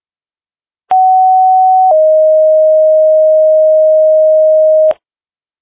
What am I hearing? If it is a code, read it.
Two-tone page, heard as a one-second tone, then a three-second tone, about 750 then 620 Hz. It is heard from 0.9 to 4.9 s.